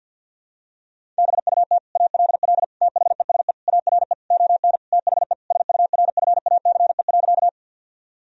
Morse code, heard as a keyed tone, about 52 wpm, 700 Hz. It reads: BUT ALL THESE ARE ON THE SURFACE.
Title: Sentences from Top 1000 Words - Review Part 3 - 50wpm